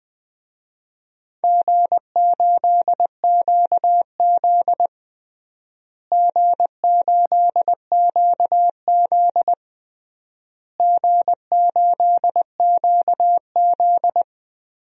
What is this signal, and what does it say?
Morse code, heard as a keyed tone, about 20 wpm, 700 Hz. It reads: G8QZ G8QZ G8QZ
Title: DXCC-100 Call Signs - 4 Characters - Repeated 3x - 20wpm